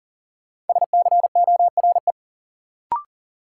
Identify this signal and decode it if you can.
Morse code, heard as a keyed tone, about 39 wpm, 700 Hz. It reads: SCORE
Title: Sentences from Top 1600 Words - New Word Drill Alt - 40wpm